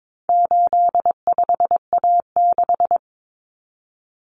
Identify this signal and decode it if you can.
Morse code, heard as a keyed tone, about 22 wpm, 700 Hz. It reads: 85A6
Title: Common Cut Numbers - 4-Digit 22wpm